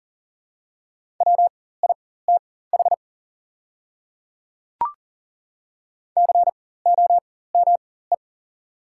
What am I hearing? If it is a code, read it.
Morse code, heard as a keyed tone, about 40 wpm, spaced as for 19 wpm, 690 Hz. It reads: WITH COME
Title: 4x Character Spacing - Top 100 Words - 40wpm